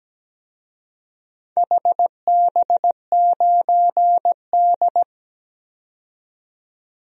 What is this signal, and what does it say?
Morse code, heard as a keyed tone, about 17 wpm, 700 Hz. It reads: HB9D
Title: DXCC-100 Call Signs - 4 Characters - 17wpm